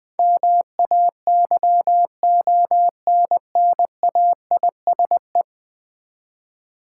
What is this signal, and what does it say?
Morse code, heard as a keyed tone, about 20 wpm, 700 Hz. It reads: MAYONNAISE
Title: Easy Words with Hints - 20wpm